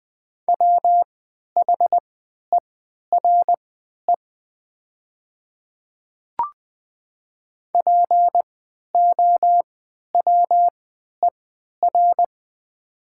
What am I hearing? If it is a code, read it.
Morse code, heard as a keyed tone, about 20 wpm, spaced as for 11 wpm, 700 Hz. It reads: WHERE POWER
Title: 3x Character Spacing - Top 500 Words - 20wpm